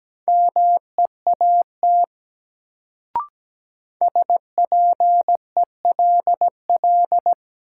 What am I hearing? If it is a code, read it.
Morse code, heard as a keyed tone, about 17 wpm, 700 Hz. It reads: MEAT SPELL
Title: Sentences from Top 1200 Words - New Word Drill Alt - 17wpm